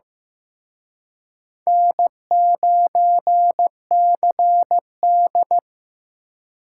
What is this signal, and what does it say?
Morse code, heard as a keyed tone, about 18 wpm, 700 Hz. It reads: N9CD
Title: Speed Racing - USA 1x2 Call Signs 15wpm